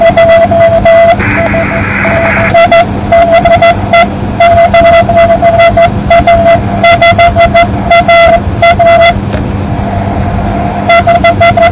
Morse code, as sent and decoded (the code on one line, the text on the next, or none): none